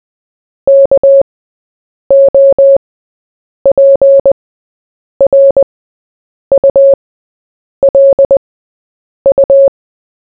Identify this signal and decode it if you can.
Morse code, heard as a keyed tone, about 20 wpm, spaced as for 8 wpm, 550 Hz. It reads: KOPRULU
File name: Short-Story-4-8wpm